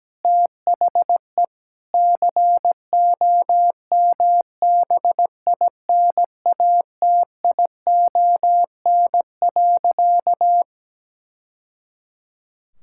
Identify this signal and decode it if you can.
Morse code, heard as a keyed tone, about 17 wpm, 700 Hz. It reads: THE COMBINATION.
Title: Sentences from Top 1000 Words - Review Part 6 - 17wpm